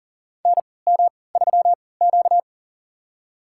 Morse code, heard as a keyed tone, about 37 wpm, spaced as for 22 wpm, 700 Hz. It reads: NM3Q
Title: USA Call Signs - 4 Characters - 3x Character Spacing - 40wpm